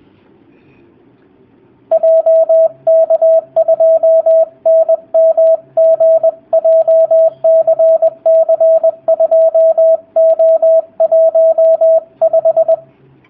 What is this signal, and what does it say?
Morse code, heard as a keyed tone, about 20 wpm, 650 Hz, with radio noise. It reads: JK2NMG JCC2O15
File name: jh8cbh / 2019cw / sp20- w